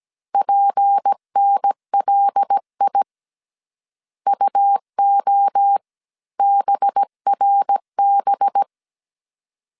Morse code, heard as a keyed tone, about 17 wpm, 800 Hz. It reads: PNLI UO BRB
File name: CW / DET / 1 6 1500C S A 16wpm 2635